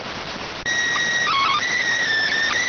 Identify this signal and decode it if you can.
SSTV (slow-scan television), PD 290.